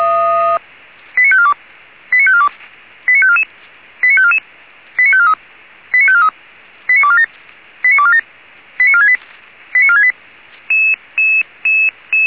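Five-tone selective call (ZVEI), heard as quick runs of five tones, tones of about 70 ms. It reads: 89542 89542 89544 89544 89543 89543 89247 89247 89468 89468
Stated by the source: ZVEI five-tone selective call received via SDR.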